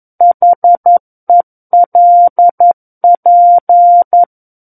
Morse code, heard as a keyed tone, about 11 wpm, 700 Hz. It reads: HELP
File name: 11 WPM